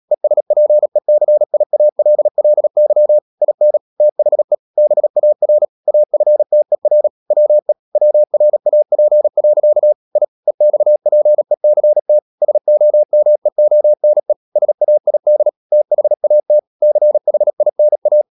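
Morse code, heard as a keyed tone, about 37 wpm, 600 Hz. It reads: ESPECIALLY IN THE BAR AFTER WE WRAP. I EXPECT SOMEONE SAID THAT CHINA
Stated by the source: Morse code practice at 35–39 wpm sending English text of Ken Follett's novel Never, from CW med Gnister.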